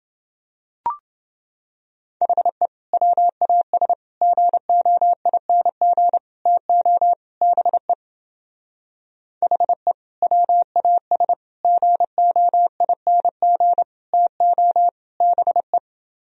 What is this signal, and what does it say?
Morse code, heard as a keyed tone, about 30 wpm, 700 Hz. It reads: HE WAS GOING TO BE HE WAS GOING TO BE